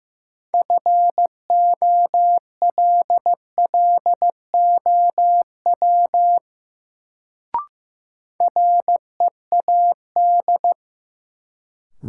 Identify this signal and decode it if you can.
Morse code, heard as a keyed tone, about 15 wpm, 700 Hz. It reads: FOLLOW READ
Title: Sentences from Top 200 Words - New Word Drill - 15wpm